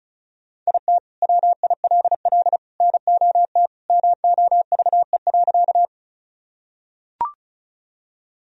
Morse code, heard as a keyed tone, about 36 wpm, 700 Hz. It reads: IT WILL NOT MOVE.
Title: Sentences from Top 1000 Words - Review Part 1 - 35wpm